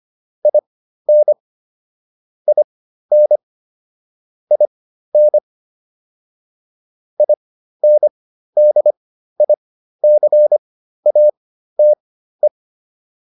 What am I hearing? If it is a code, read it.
Morse code, heard as a keyed tone, about 25 wpm, spaced as for 13 wpm, 600 Hz. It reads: IN IN IN INDICATE